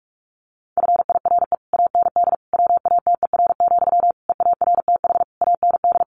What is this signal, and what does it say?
Morse code, heard as a keyed tone, about 44 wpm, 700 Hz. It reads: FIRE AND WATER, EARTH AND
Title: Sentences from Top 400 Words 45wpm